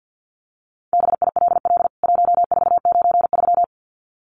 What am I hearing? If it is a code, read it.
Morse code, heard as a keyed tone, about 48 wpm, 700 Hz. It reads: BILL 1493
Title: CWT Exchange 50wpm